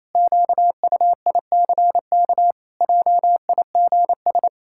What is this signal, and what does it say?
Morse code, heard as a keyed tone, about 27 wpm, 700 Hz. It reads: QUICK JIGS
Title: Pangram Challenge v2 - 28wpm